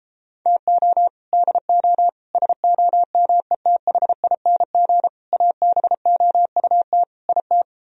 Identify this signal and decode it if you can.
Morse code, heard as a keyed tone, about 33 wpm, 700 Hz. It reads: TO DO SOMETHING ABOUT IT